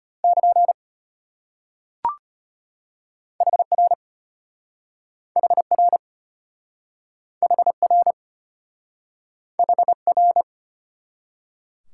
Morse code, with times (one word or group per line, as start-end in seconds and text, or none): 0.2-0.7 <KN>
3.4-3.9 HR
5.4-6.0 HR
7.4-8.1 HR
9.6-10.4 HR